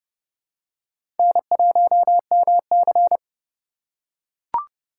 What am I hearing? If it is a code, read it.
Morse code, heard as a keyed tone, about 30 wpm, 700 Hz. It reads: N1MC
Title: USA Call Signs - 4 Characters - 3x Character Spacing - 30wpm